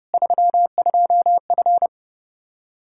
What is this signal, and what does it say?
Morse code, heard as a keyed tone, about 32 wpm, 700 Hz. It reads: 32F
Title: K1USN SST QSO Practice 30wpm